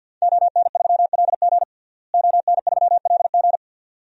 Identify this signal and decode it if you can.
Morse code, heard as a keyed tone, about 52 wpm, 700 Hz. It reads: ON3LG ON3LG
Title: DXCC-100 Call Signs - 5 Characters - Repeated 2x - 50wpm